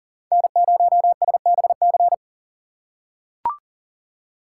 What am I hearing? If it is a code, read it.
Morse code, heard as a keyed tone, about 40 wpm, 700 Hz. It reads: N0SBC
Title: USA Call Signs - 5 Characters - 4x Character Spacing - 40wpm